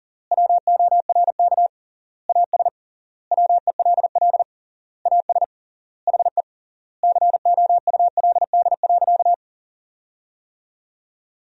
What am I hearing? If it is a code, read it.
Morse code, heard as a keyed tone, about 40 wpm, 700 Hz. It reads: WORK AS WELL AS HE COULD.
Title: 3x Word-Spacing - Sentences from Top 500 Words - 40wpm